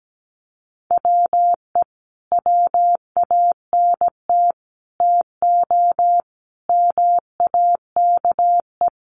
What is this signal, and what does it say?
Morse code, heard as a keyed tone, about 17 wpm, 700 Hz. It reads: WE WANT TO MAKE
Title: Sets of 5 Words Encore2 17wpm